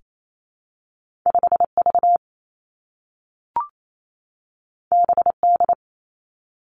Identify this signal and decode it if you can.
Morse code, heard as a keyed tone, about 28 wpm, 700 Hz. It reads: 5 V B D
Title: Two Challenging Letter-Number Combinations 28wpm